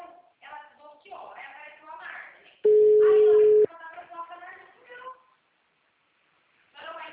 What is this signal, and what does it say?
Ringback tone (Europe, 425 Hz), with speech.
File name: out- -2009- - - .42915